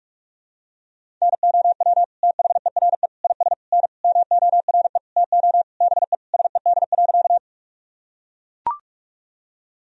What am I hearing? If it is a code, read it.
Morse code, heard as a keyed tone, about 41 wpm, 700 Hz. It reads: NOW THERE IS N MORE TO BE SED.
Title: Sentences from Top 100 Words and Abbreviations 45wpm